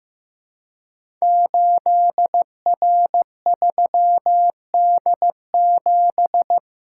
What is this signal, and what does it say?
Morse code, heard as a keyed tone, about 15 wpm, 700 Hz. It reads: 8R3D7
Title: Code Groups - 5 Letter:Numbers - 15wpm